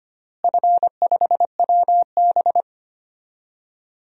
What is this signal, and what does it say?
Morse code, heard as a keyed tone, about 25 wpm, 700 Hz. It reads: F5WB
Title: DXCC-100 Call Signs - 4 Characters - Repeated 3x - 25wpm